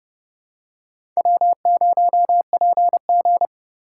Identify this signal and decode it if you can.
Morse code, heard as a keyed tone, about 29 wpm, 700 Hz. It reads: W0PG
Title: USA Call Signs - 4 Characters - 3x Character Spacing - 30wpm